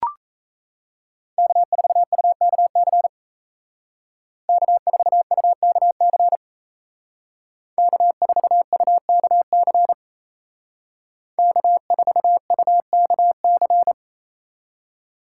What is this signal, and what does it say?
Morse code, heard as a keyed tone, about 36 wpm, 700 Hz. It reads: K4UKC K4UKC K4UKC K4UKC